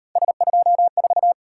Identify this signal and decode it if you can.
Morse code, heard as a keyed tone, about 38 wpm, 700 Hz. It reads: S24